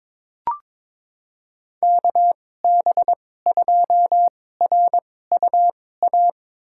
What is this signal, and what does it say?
Morse code, heard as a keyed tone, about 22 wpm, spaced as for 16 wpm, 700 Hz. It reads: KB2RUA
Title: USA Call Signs - 2x Character Spacing - 22wpm